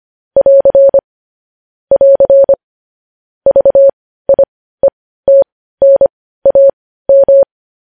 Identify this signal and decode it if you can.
Morse code, heard as a keyed tone, about 25 wpm, spaced as for 15 wpm, 550 Hz. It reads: <AR> <AR> VIETNAM